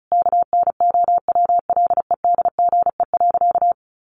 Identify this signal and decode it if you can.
Morse code, heard as a keyed tone, about 34 wpm, 700 Hz. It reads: KNOWLEDGE.
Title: Sentences from Top 1200 Words 35wpm